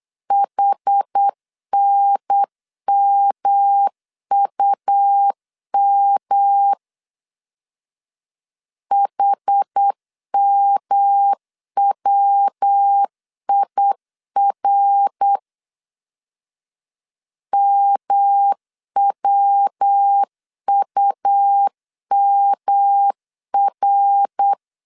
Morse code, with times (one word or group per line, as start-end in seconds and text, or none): 0.3-6.7 HNMUM
8.9-15.4 HMWIR
17.5-24.5 MWUMR